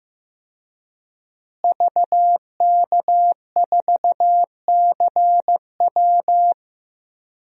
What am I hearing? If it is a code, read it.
Morse code, heard as a keyed tone, about 15 wpm, 700 Hz. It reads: VK4CW
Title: DXCC-100 Call Signs - 5 Characters - Repeated 2x - 15wpm